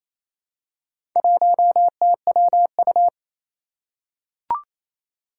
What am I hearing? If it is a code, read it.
Morse code, heard as a keyed tone, about 27 wpm, 700 Hz. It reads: 1TWU
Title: Four Challenging Letter-Number Combinations 28wpm